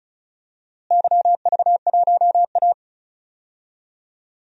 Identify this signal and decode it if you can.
Morse code, heard as a keyed tone, about 34 wpm, 700 Hz. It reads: YV1A
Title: DXCC-100 Call Signs - 4 Characters - Repeated 2x - 35wpm